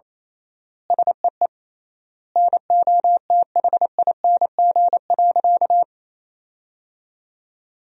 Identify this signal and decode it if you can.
Morse code, heard as a keyed tone, about 28 wpm, 700 Hz. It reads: SEE NOTHING.
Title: 3x Word-Spacing - Sentences from Top 500 Words - 28wpm